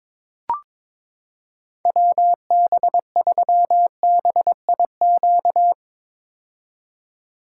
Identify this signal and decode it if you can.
Morse code, heard as a keyed tone, about 22 wpm, 700 Hz. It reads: WB3BIQ
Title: K1USN SST QSO Practice - 22wpm